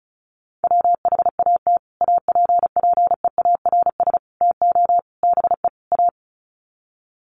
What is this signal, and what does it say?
Morse code, heard as a keyed tone, about 34 wpm, 700 Hz. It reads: WHAT APPEARS TO BE A
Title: Sets of 5 Words Encore2 35wpm